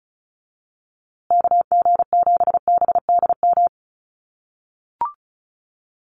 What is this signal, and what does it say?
Morse code, heard as a keyed tone, about 34 wpm, 700 Hz. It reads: KG7BDM
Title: USA Call Signs Repeated 2x 35wpm